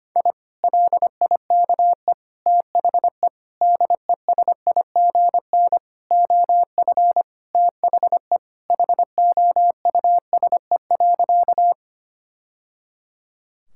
Morse code, heard as a keyed tone, about 25 wpm, 700 Hz. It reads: I LIKE THE DESIGN OF THE HOUSE.
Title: Sentences from Top 1000 Words - Review Part 5 - 25wpm